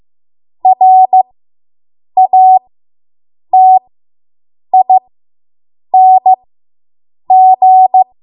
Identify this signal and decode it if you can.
Morse code, heard as a keyed tone, about 15 wpm, spaced as for 7 wpm, 750 Hz. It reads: RATING